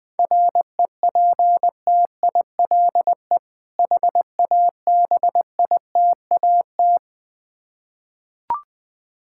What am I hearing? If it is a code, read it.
Morse code, heard as a keyed tone, about 21 wpm, 700 Hz. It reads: REPTILE HABITAT
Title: Difficult Words with Hints - 20wpm